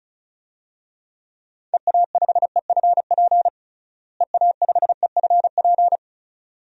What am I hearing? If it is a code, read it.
Morse code, heard as a keyed tone, about 34 wpm, 700 Hz. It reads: EA5EFP EA5EFP
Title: DXCC-100 Call Signs - 6 Characters - Repeated 2x - 35wpm